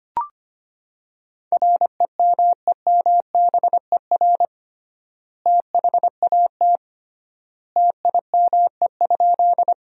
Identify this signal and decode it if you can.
Morse code, heard as a keyed tone, about 25 wpm, 700 Hz. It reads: REMEMBER THAT TIME?
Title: 3x Word-Spacing - Sentences from Top 400 Words - 25wpm